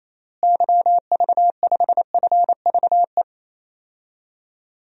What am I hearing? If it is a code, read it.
Morse code, heard as a keyed tone, about 27 wpm, 700 Hz. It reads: YV5FVE
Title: DXCC-100 Call Signs - 6 Characters - Repeated 3x - 28wpm